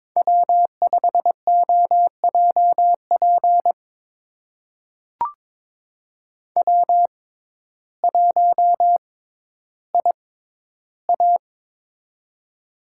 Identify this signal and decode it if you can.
Morse code, heard as a keyed tone, about 22 wpm, spaced as for 12 wpm, 700 Hz. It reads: W5OJP W1IA
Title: USA Call Signs - 6x Character Spacing - 22wpm